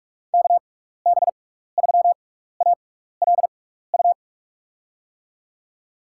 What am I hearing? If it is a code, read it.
Morse code, heard as a keyed tone, about 48 wpm, spaced as for 16 wpm, 700 Hz. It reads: KB3ALU
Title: USA Call Signs - 6x Character Spacing - 45wpm